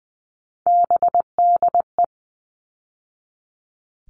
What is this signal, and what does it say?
Morse code, heard as a keyed tone, about 20 wpm, 700 Hz. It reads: BDE
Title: All Cut Numbers - 3-Digit 20wpm